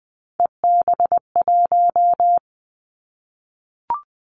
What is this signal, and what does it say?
Morse code, heard as a keyed tone, about 20 wpm, 700 Hz. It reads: EB1
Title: Callsign Prefixes for Top 25 DXCC - Repeated 3x - 20wpm